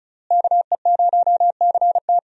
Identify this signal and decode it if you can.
Morse code, heard as a keyed tone, about 34 wpm, 700 Hz. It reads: KE0CT
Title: USA Call Signs - 5 Characters - Repeated 3x - 35wpm